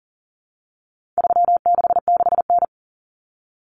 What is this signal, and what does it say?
Morse code, heard as a keyed tone, about 40 wpm, 700 Hz. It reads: 366N
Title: Common Cut Numbers - 4-Digit 40wpm